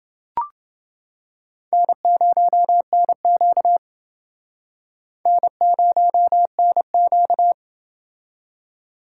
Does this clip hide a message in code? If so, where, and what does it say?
Yes: Morse code from 1.7 to 7.5 s, heard as a keyed tone, about 29 wpm, 700 Hz. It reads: N0NQ N0NQ